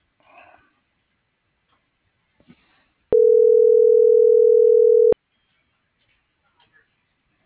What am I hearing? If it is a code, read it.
Ringback tone (North America), 440 + 480 Hz.